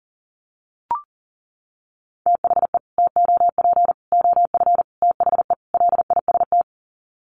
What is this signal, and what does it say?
Morse code, heard as a keyed tone, about 40 wpm, 700 Hz. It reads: THE TOP OF THE LIST
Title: Sets of 5 Words Encore2 40wpm